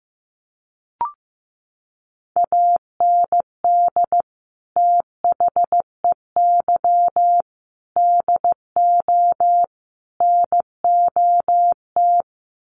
Morse code, heard as a keyed tone, about 15 wpm, 700 Hz. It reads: AND THEY DO NOT
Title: Sets of 4 Words 15wpm